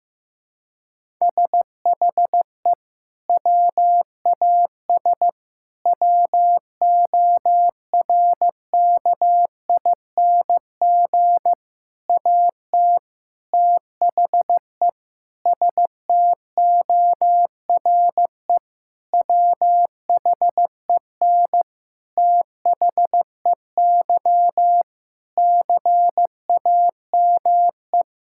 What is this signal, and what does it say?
Morse code, heard as a keyed tone, about 16 wpm, 700 Hz. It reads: SHE WAS WORKING AT THE STORE WHEN THEY CAME
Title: Sentences from Top 1000 Words - Review Part 5 - 15wpm